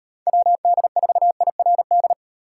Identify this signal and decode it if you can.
Morse code, heard as a keyed tone, about 38 wpm, 700 Hz. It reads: WD4IRD